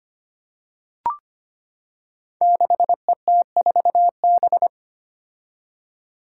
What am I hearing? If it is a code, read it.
Morse code, heard as a keyed tone, about 25 wpm, 700 Hz. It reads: 6ET4B
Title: Five Challenging Letter-Number Combinations 25wpm